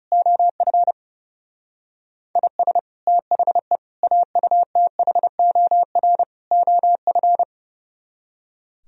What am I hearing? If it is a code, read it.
Morse code, heard as a keyed tone, about 39 wpm, 700 Hz. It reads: OF IS THE AUTHOR OF